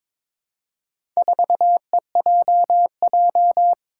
Morse code, heard as a keyed tone, about 21 wpm, 700 Hz. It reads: 4EJJ
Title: Four Challenging Letter-Number Combinations 22wpm